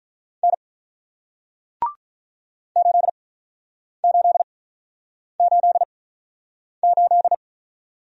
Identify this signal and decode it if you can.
Morse code, heard as a keyed tone, about 48 wpm, spaced as for 9 wpm, 700 Hz. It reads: N 8888